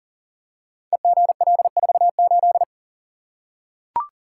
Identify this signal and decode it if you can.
Morse code, heard as a keyed tone, about 41 wpm, 690 Hz. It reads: EGL48
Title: Code Groups - 5 Letter:Numbers - 40wpm